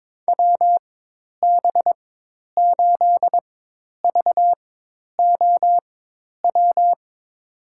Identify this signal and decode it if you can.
Morse code, heard as a keyed tone, about 22 wpm, spaced as for 10 wpm, 700 Hz. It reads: WB8VOW
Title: USA Call Signs - 6 Characters - 4x Character Spacing - 22wpm